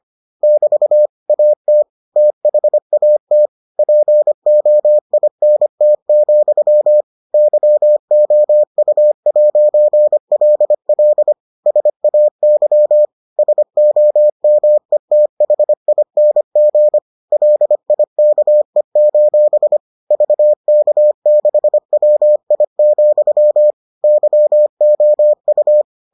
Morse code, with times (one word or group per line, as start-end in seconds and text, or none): 0.4-1.1 <BT>
1.3-1.8 AT
2.2-3.5 THAT
3.8-7.0 POINT,
7.3-11.3 YOU'LL
11.7-13.1 SAY
13.4-17.0 SOMETHING
17.3-19.8 LIKE:
20.1-23.7 VK6WI,
24.0-25.8 YOU